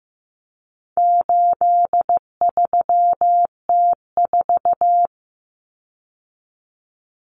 Morse code, heard as a keyed tone, about 15 wpm, 700 Hz. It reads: 83T4
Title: Common Cut Numbers - 4-Digit 15wpm